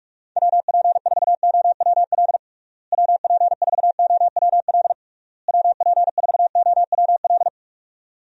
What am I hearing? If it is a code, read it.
Morse code, heard as a keyed tone, about 41 wpm, 700 Hz. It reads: WP4OWL WP4OWL WP4OWL
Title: USA Call Signs - 6 Characters - Repeated 3x - 45wpm